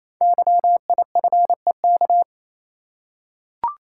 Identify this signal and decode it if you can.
Morse code, heard as a keyed tone, about 28 wpm, 700 Hz. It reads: YIFEK
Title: Code Groups - 5 Letters - 28wpm